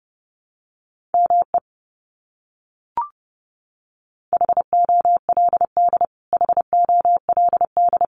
Morse code, heard as a keyed tone, about 30 wpm, 700 Hz. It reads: ME HOLD HOLD
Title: Top 500 Words Repeated 30wpm